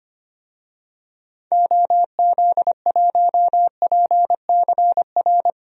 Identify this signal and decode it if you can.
Morse code, heard as a keyed tone, about 25 wpm, 700 Hz. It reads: OZ1PCR
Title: DXCC-100 Call Signs - 6 Characters - Repeated 2x - 25wpm